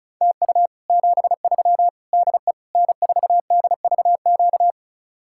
Morse code, heard as a keyed tone, about 46 wpm, 700 Hz. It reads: TU 73 DE N4DVQ